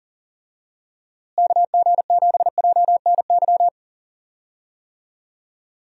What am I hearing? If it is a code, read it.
Morse code, heard as a keyed tone, about 40 wpm, 700 Hz. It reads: KG7JNY
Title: USA Call Signs - 6 Characters - Repeated 2x - 40wpm